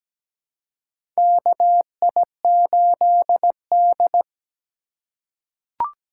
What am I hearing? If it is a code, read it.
Morse code, heard as a keyed tone, about 17 wpm, 700 Hz. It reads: KI8D
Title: USA Call Signs - 4 Characters - 4x Character Spacing - 17wpm